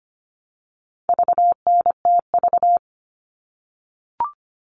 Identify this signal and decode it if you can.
Morse code, heard as a keyed tone, about 25 wpm, 700 Hz. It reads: VNTV
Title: All Cut Numbers - 4-Digit 25wpm